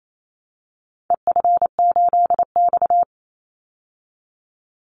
Morse code, heard as a keyed tone, about 28 wpm, 700 Hz. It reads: EF8X
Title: Call Signs Repeated 2x - 28wpm